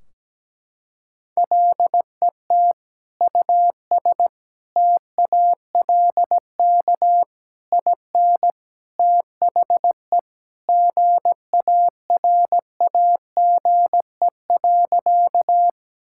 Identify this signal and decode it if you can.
Morse code, heard as a keyed tone, about 17 wpm, 700 Hz. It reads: LET US TALK IN THE GARAGE.
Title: Sentences from Top 1600 Words 17wpm